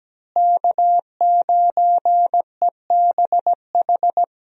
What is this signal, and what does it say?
Morse code, heard as a keyed tone, about 17 wpm, 700 Hz. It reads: K9EBH
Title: USA Call Signs - 5 Characters - Repeated 2x - 17wpm